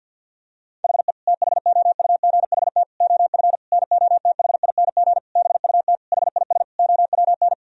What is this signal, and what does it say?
Morse code, heard as a keyed tone, about 50 wpm, 700 Hz. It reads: HE THOUGHT OF NOTHING BUT HIS OWN
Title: Sentences from Top 1000 Words - Review Part 4 - 50wpm